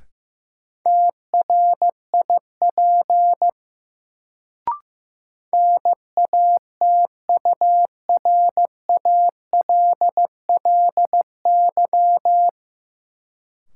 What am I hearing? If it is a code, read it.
Morse code, heard as a keyed tone, about 15 wpm, 700 Hz. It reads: TRIP NATURALLY
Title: Sentences from Top 1300 Words - New Word Drill Alt - 15wpm